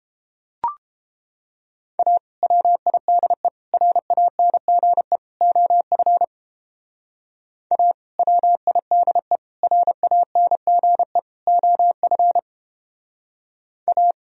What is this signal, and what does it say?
Morse code, heard as a keyed tone, about 29 wpm, 700 Hz. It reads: A WIDE RANGE OF A WIDE RANGE OF A